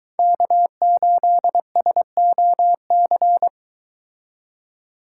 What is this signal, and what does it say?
Morse code, heard as a keyed tone, about 23 wpm, 700 Hz. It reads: K8SOC